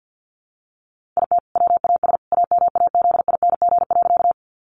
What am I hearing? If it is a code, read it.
Morse code, heard as a keyed tone, about 50 wpm, 700 Hz. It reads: IT WAS AMAZING.
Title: Sentences from Top 1500 Words 50wpm